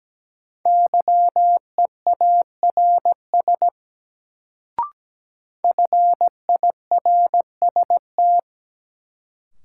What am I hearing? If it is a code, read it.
Morse code, heard as a keyed tone, about 17 wpm, 700 Hz. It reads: YEARS FIRST
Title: Sets of 2 Words - New Word Drill - 17wpm